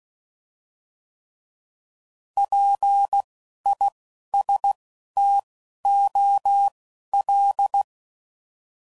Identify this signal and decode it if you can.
Morse code, heard as a keyed tone, about 16 wpm, spaced as for 12 wpm, 800 Hz. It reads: PISTOL
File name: Prefixes / By First Letter / Ps